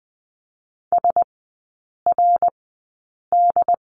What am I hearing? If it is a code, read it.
Morse code, heard as a keyed tone, about 20 wpm, spaced as for 8 wpm, 700 Hz. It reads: SRD
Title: 067 Lesson 12e - Letter d - Three Characters 20wpm